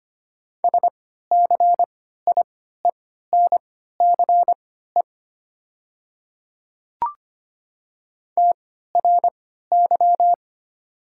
Morse code, heard as a keyed tone, about 25 wpm, spaced as for 14 wpm, 700 Hz. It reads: SCIENCE TRY
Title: 3x Character Spacing - Top 500 Words - 25wpm